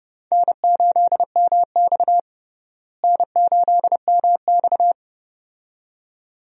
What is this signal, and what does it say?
Morse code, heard as a keyed tone, about 30 wpm, 700 Hz. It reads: N8MX N8MX
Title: USA Call Signs - 4 Characters - Repeated 2x - 30wpm